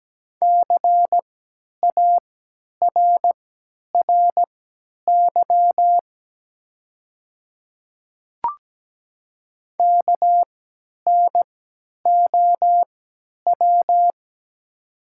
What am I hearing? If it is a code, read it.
Morse code, heard as a keyed tone, about 17 wpm, spaced as for 10 wpm, 690 Hz. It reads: CARRY KNOW
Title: 3x Character Spacing - Top 500 Words - 17wpm